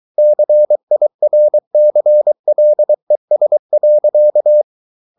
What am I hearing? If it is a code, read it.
Morse code, heard as a keyed tone, about 23 wpm, 600 Hz. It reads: CIRCLES.